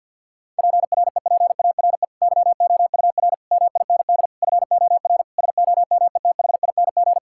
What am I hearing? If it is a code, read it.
Morse code, heard as a keyed tone, about 50 wpm, 700 Hz. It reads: PREPARE YOUR MIND FOR SOMETHING
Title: Sentences from Top 1000 Words - Review Part 6 - 50wpm